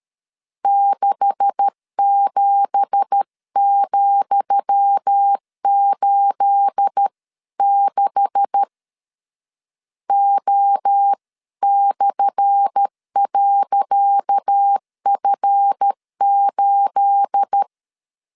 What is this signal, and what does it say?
Morse code, heard as a keyed tone, about 13 wpm, 800 Hz. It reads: 67,86 O/.F8
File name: LEZIONI / 10 SIGNS / 10D COMA 12wpm 2544